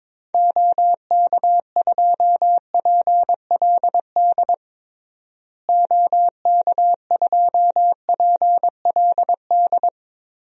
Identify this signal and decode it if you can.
Morse code, heard as a keyed tone, about 22 wpm, 700 Hz. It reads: OK2PLD OK2PLD
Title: DXCC-100 Call Signs - 6 Characters - Repeated 3x - 22wpm